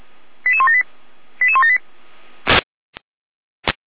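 Five-tone selective call (ZVEI), heard as quick runs of five tones, tones of about 70 ms. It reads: 88178 88178
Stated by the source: Five-tone ZVEI selective call received by SDR, from Feuerwehr Orlamünde server.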